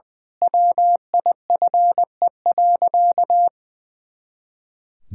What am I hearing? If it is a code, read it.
Morse code, heard as a keyed tone, about 21 wpm, 700 Hz. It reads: WIFE.
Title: Pangram Challenge v2 - 20wpm